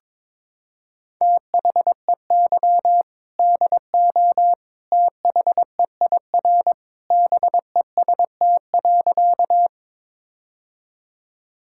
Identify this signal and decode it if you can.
Morse code, heard as a keyed tone, about 22 wpm, 700 Hz. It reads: THEY DO THEIR BEST.
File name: practice / 22wpm / Sentences with 2-Word Phrases - 22wpm